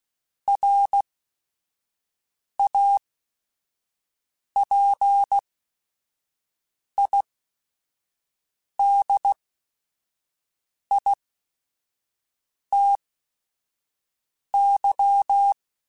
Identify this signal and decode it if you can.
Morse code, heard as a keyed tone, about 16 wpm, spaced as for 5 wpm, 800 Hz. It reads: RAPIDITY
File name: Technical Subjects / Technical Subjects-mp3s / Reactance @5